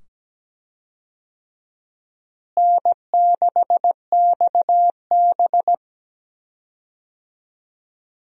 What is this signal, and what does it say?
Morse code, heard as a keyed tone, about 17 wpm, 700 Hz. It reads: N6XB
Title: USA Call Signs - 4 Characters - Repeated 2x - 17wpm